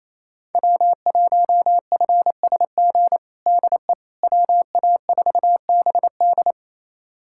Morse code, heard as a keyed tone, about 28 wpm, 700 Hz. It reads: W1FSG DE WA4BD
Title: USA Callsign DE Callsign - 5 Characters - 28wpm